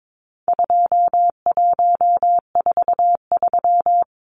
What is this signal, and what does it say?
Morse code, heard as a keyed tone, about 22 wpm, 700 Hz. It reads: 2143
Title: Numbers - 4-Digit 22wpm